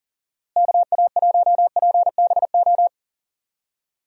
Morse code, heard as a keyed tone, about 38 wpm, 700 Hz. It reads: KA1PBO
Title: USA Call Signs - 6 Characters - Repeated 2x - 40wpm